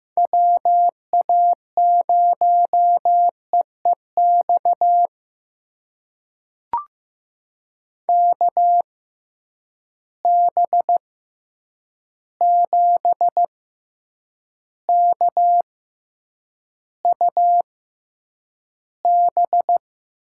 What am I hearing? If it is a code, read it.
Morse code, heard as a keyed tone, about 15 wpm, spaced as for 8 wpm, 700 Hz. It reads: WA0EEX KB7KUB
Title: USA Call Signs - 6 Characters - 6x Character Spacing - 15wpm